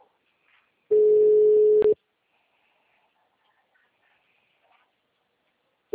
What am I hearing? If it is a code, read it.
Ringback tone (Europe, 425 Hz).
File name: out- -2014- - - .3074